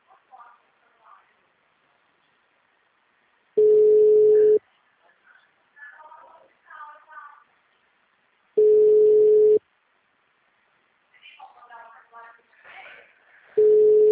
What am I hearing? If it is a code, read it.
Ringback tone (Europe, 425 Hz), with speech.
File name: out- -2009- - - .366